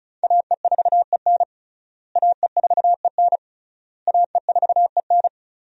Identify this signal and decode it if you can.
Morse code, heard as a keyed tone, about 36 wpm, 700 Hz. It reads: AE4EN AE4EN AE4EN
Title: USA Call Signs - 5 Characters - Repeated 3x - 35wpm